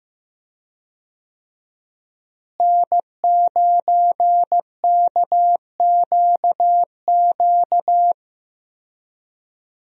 Morse code, heard as a keyed tone, about 15 wpm, 700 Hz. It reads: N9KQQ
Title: USA Call Signs - 5 Characters - Repeated 2x - 15wpm